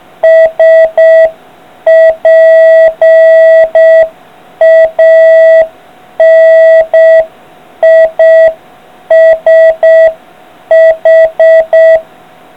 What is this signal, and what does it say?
Morse code, heard as a keyed tone, about 6 wpm, 650 Hz. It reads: SPANISH